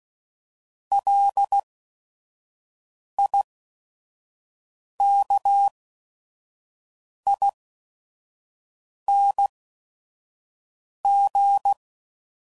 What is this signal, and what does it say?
Morse code, heard as a keyed tone, about 16 wpm, spaced as for 5 wpm, 800 Hz. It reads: LIKING